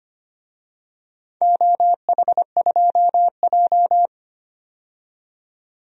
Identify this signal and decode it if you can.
Morse code, heard as a keyed tone, about 25 wpm, 700 Hz. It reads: OH2J
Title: DXCC-100 Call Signs - 4 Characters - 25wpm